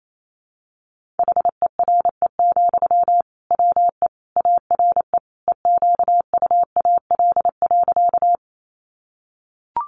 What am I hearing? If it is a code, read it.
Morse code, heard as a keyed tone, about 28 wpm, 700 Hz. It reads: HERE, WE ARE EQUAL.
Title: Sentences from Top 1400 Words 28wpm